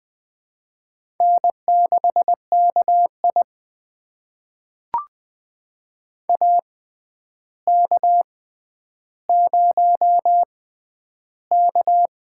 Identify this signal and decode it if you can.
Morse code, heard as a keyed tone, about 20 wpm, spaced as for 10 wpm, 700 Hz. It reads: N6KI AK0K
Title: USA Call Signs - 4 Characters - 6x Character Spacing - 20wpm